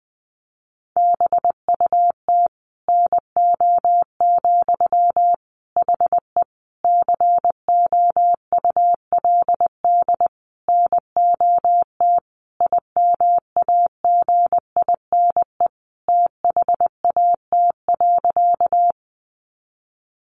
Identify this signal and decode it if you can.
Morse code, heard as a keyed tone, about 20 wpm, 700 Hz. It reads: BUT NO, HE COULD NOT IMAGINE THAT.
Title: Sentences from Top 1400 Words 20wpm